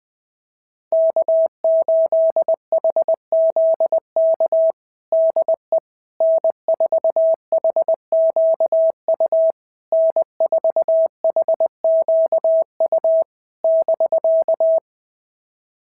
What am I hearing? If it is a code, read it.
Morse code, heard as a keyed tone, about 20 wpm, 650 Hz. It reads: K8HZK DE N4HQU N4HQU <BK>
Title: SOTA QSO Practice 20wpm